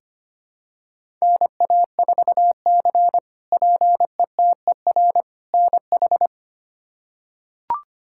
Morse code, heard as a keyed tone, about 25 wpm, 700 Hz. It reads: NA4C PETER NH
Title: K1USN SST QSO Practice - 25wpm